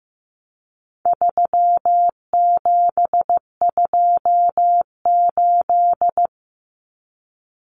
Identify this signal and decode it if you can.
Morse code, heard as a keyed tone, about 15 wpm, 700 Hz. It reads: 3728
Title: Common Cut Numbers - 4-Digit 15wpm